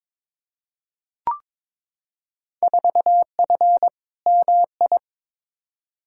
Morse code, heard as a keyed tone, about 21 wpm, 700 Hz. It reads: 4F MI
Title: Field Day Exchanges Encore - 22wpm